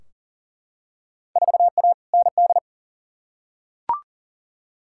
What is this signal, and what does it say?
Morse code, heard as a keyed tone, about 40 wpm, 700 Hz. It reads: 4A ND
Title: Field Day Exchanges Encore - 40wpm